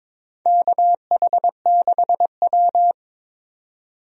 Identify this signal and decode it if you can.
Morse code, heard as a keyed tone, about 22 wpm, 700 Hz. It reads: KH6W
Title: DXCC-100 Call Signs - 4 Characters - 22wpm